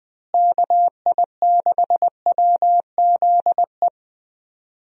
Morse code, heard as a keyed tone, about 20 wpm, 700 Hz. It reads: KI6WZE
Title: USA Call Signs - 6 Characters - Repeated 2x - 20wpm